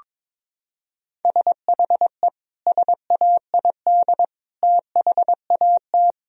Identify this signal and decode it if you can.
Morse code, heard as a keyed tone, about 22 wpm, 700 Hz. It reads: SHE SAID THAT
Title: Sentences from Top 1000 Words - Review Part 4 - 22wpm